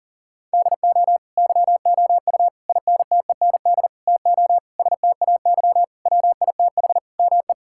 Morse code, heard as a keyed tone, about 40 wpm, 700 Hz. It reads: DO YOU INTEND TO STAY WITH ME
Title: Sentences from Top 1600 Words 40wpm